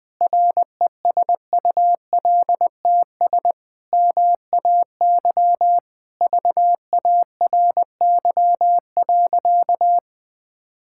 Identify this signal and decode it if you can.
Morse code, heard as a keyed tone, about 20 wpm, 700 Hz. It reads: RESULTS MAY VARY.
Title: Sentences with 6-Word Phrases - 20wpm